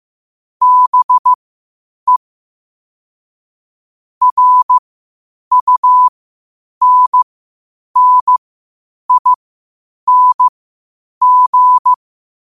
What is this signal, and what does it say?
Morse code, heard as a keyed tone, about 15 wpm, spaced as for 8 wpm, 1000 Hz. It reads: BE RUNNING
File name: qotdfiles / 8wpm